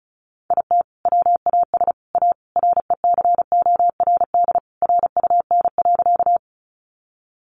Morse code, heard as a keyed tone, about 34 wpm, 700 Hz. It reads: IT WAS A RECORD RUN.
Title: Sentences from Top 500 Words 35wpm